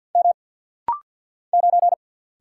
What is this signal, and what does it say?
Morse code, heard as a keyed tone, about 50 wpm, 700 Hz. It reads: M 9
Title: Single Letter-Number - Rapid-Fire 50wpm